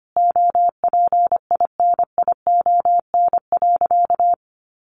Morse code, heard as a keyed tone, about 25 wpm, 700 Hz. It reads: OPINION.
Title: Sentences from Top 1200 Words 25wpm